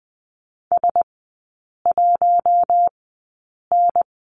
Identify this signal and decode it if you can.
Morse code, heard as a keyed tone, about 20 wpm, spaced as for 8 wpm, 700 Hz. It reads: S1N
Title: 053 Lesson 10e - Letter r - Three Characters 20wpm